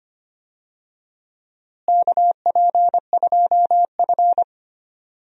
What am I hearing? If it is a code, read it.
Morse code, heard as a keyed tone, about 25 wpm, 700 Hz. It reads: KP2F
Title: DXCC-100 Call Signs - 4 Characters - Repeated 3x - 25wpm